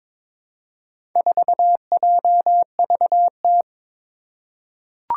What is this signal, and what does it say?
Morse code, heard as a keyed tone, about 22 wpm, 700 Hz. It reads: 4JVT
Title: Four Challenging Letter-Number Combinations 22wpm